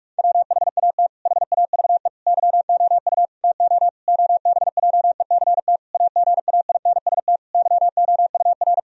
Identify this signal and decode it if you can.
Morse code, heard as a keyed tone, about 41 wpm, 700 Hz. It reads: WHAT HAVE YOU TO OBJECT AGAINST YOUR
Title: Sentences from Top 1000 Words - Review Part 3 - 45wpm